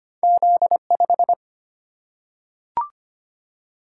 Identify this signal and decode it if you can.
Morse code, heard as a keyed tone, about 25 wpm, 690 Hz. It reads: Z5
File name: Two Letter-Number Combinations - 25wpm